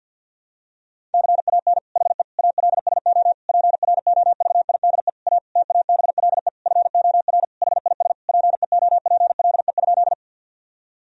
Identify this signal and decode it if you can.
Morse code, heard as a keyed tone, about 50 wpm, 700 Hz. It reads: CAN HE ALSO PROVIDE A TABLE FOR HIS PEOPLE?
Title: Sentences from Top 1000 Words - Review Part 3 - 50wpm